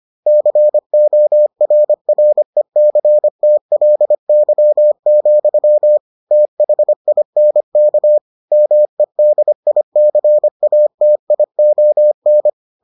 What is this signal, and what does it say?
Morse code, heard as a keyed tone, about 25 wpm, 600 Hz. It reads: CORRECTLY, THINK MEDICATION